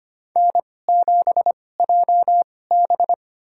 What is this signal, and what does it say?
Morse code, heard as a keyed tone, about 25 wpm, spaced as for 18 wpm, 700 Hz. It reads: N7JB
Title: USA Call Signs - 4 Characters - 2x Character Spacing - 25wpm